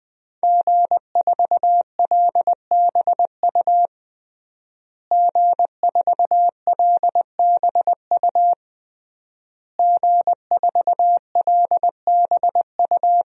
Morse code, heard as a keyed tone, about 20 wpm, 700 Hz. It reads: G4LBU G4LBU G4LBU
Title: DXCC-100 Call Signs - 5 Characters - Repeated 3x - 20wpm